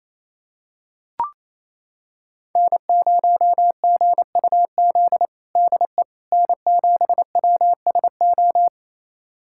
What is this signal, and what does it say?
Morse code, heard as a keyed tone, about 28 wpm, 700 Hz. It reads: N0GUZ DE N7WSO
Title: USA Callsign DE Callsign - 5 Characters - 28wpm